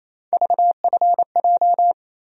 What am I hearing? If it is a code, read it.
Morse code, heard as a keyed tone, about 28 wpm, 700 Hz. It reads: VFJ
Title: USA Call Sign Postfixes - 3 Letters 28wpm